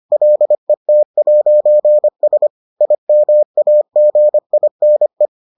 Morse code, heard as a keyed tone, about 25 wpm, 600 Hz. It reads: LET'S IMAGINE